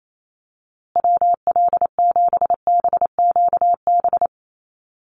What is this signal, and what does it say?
Morse code, heard as a keyed tone, about 28 wpm, 690 Hz. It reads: WL7BQB
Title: USA Call Signs Repeated 3x 28wpm